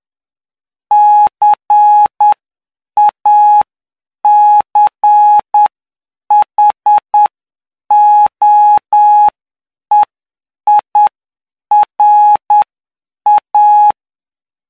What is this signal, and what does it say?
Morse code, heard as a keyed tone, about 9 wpm, spaced as for 7 wpm, 840 Hz. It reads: CACHOEIRA
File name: cw / py2lcd / 09-PY2LCD-Li��o 18